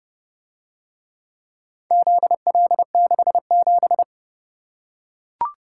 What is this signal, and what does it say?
Morse code, heard as a keyed tone, about 29 wpm, 700 Hz. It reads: ZL67